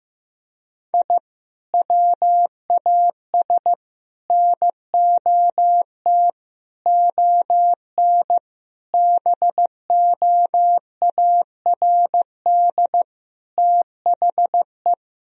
Morse code, heard as a keyed tone, about 15 wpm, 700 Hz. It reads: I WAS NOT ON BOARD THE
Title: Sentences from Top 1000 Words - Review Part 4 - 15wpm